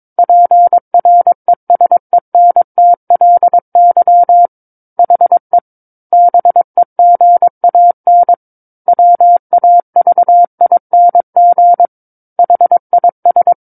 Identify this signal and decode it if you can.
Morse code, heard as a keyed tone, about 22 wpm, 700 Hz. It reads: PRESENTLY HE BEGAN WAVING HIS
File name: Book One / 13-Chapter 13 - 22 WPM